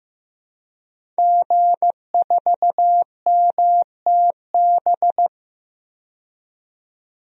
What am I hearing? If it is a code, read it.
Morse code, heard as a keyed tone, about 15 wpm, 700 Hz. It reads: G4MTB
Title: DXCC-100 Call Signs - 5 Characters - 15wpm